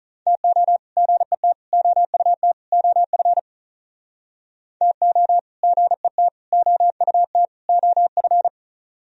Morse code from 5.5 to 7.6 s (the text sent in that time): GET OUT